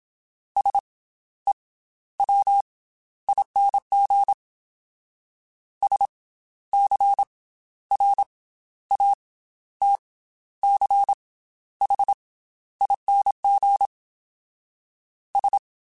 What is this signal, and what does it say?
Morse code, heard as a keyed tone, about 27 wpm, spaced as for 13 wpm, 800 Hz. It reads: SEWING SCRATCHING S
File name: Special INGs / Special INGs-mp3s / Slow ing @25